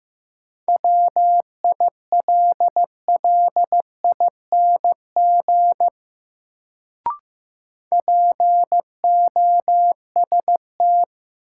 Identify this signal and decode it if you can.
Morse code, heard as a keyed tone, about 15 wpm, 700 Hz. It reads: WILLING POST
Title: Sentences from Top 1000 Words - New Word Drill - 15wpm